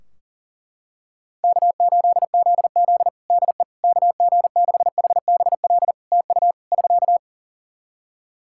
Morse code, heard as a keyed tone, about 40 wpm, 700 Hz. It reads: K8ZZ DE KG6HBL TU <SK>
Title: Practice QSO - Exchange 4 - Ending - 40wpm